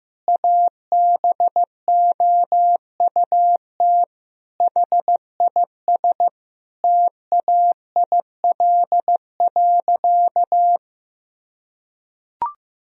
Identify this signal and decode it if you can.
Morse code, heard as a keyed tone, about 15 wpm, 700 Hz. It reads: ABOUT HIS TAIL.
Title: Sentences from Top 1000 Words - Review Part 3 - 15wpm